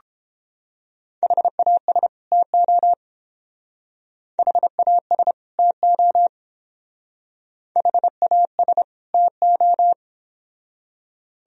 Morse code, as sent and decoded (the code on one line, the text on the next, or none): .... .- ... / - --- / .... .- ... / - --- / .... .- ... / - ---
HAS TO HAS TO HAS TO